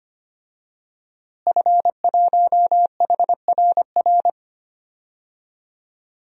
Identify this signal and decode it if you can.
Morse code, heard as a keyed tone, about 25 wpm, 700 Hz. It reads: F1HRR
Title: DXCC-100 Call Signs - 5 Characters - Repeated 3x - 25wpm